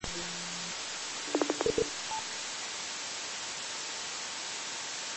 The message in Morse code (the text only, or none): none